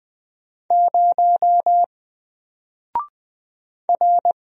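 Morse code, heard as a keyed tone, about 20 wpm, 700 Hz. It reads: 0 R
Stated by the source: Single letter and number Morse drill at 20 wpm, from N4LCD.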